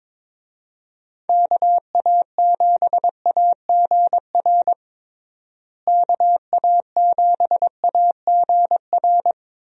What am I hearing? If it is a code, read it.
Morse code, heard as a keyed tone, about 22 wpm, 700 Hz. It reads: KA7AGR KA7AGR
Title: USA Call Signs - 6 Characters - Repeated 2x - 22wpm